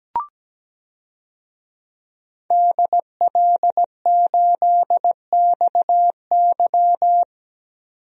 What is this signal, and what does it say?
Morse code, heard as a keyed tone, about 17 wpm, 700 Hz. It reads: DL8XY
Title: DXCC-100 Call Signs - 5 Characters - 17wpm